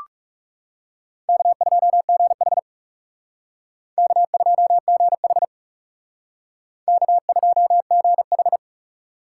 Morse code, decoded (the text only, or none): K2GH K2GH K2GH